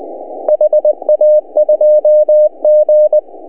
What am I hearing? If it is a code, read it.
Morse code, heard as a keyed tone, about 21 wpm, 630 Hz. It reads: HA2G
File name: CW / HA2OS RHOB 2017 CW